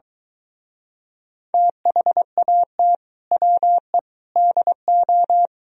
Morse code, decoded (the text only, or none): THAT WE DO